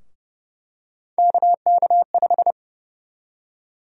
Morse code, heard as a keyed tone, about 30 wpm, 700 Hz. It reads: KK5
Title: USA Call Sign Prefixes - 30wpm